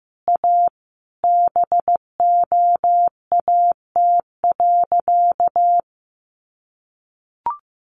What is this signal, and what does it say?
Morse code, heard as a keyed tone, about 15 wpm, 700 Hz. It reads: A BOAT.
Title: Sentences from Top 1500 Words 15wpm